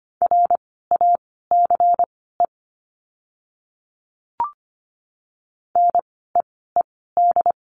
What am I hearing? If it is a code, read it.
Morse code, heard as a keyed tone, about 25 wpm, spaced as for 16 wpm, 700 Hz. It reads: RACE NEED
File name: course / 25:10wpm / 094 Lesson 16f - Letter c - Words 25wpm